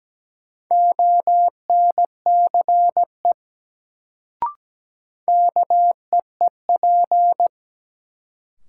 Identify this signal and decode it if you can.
Morse code, heard as a keyed tone, about 17 wpm, 700 Hz. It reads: ONCE KEEP
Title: Sentences from Top 300 Words - New Word Drill - 17wpm